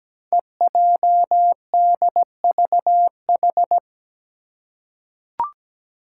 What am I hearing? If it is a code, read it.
Morse code, heard as a keyed tone, about 17 wpm, 700 Hz. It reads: EJDVH
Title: Five Challenging Letter-Number Combinations 17wpm